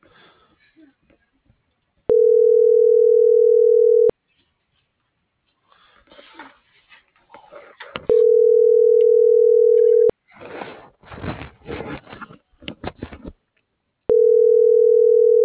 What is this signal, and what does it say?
Ringback tone (North America), 440 + 480 Hz.